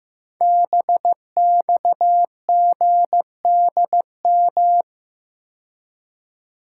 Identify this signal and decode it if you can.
Morse code, heard as a keyed tone, about 15 wpm, 700 Hz. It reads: BXGDM
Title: Code Groups - 5 Letters - 15wpm